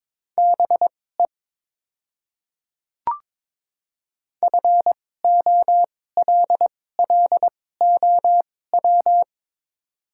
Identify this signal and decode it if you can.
Morse code, heard as a keyed tone, about 22 wpm, spaced as for 16 wpm, 700 Hz. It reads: BE FOLLOW
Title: 2x Character Spacing - Top 200 Words - 22wpm